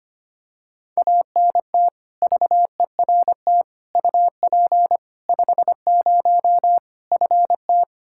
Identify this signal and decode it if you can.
Morse code, heard as a keyed tone, about 25 wpm, 700 Hz. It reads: ANT VERT UP 50 FT
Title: Practice QSO - Exchange 2 - Info - 25wpm